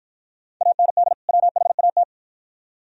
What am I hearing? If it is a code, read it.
Morse code, heard as a keyed tone, about 50 wpm, 700 Hz. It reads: AND WHAT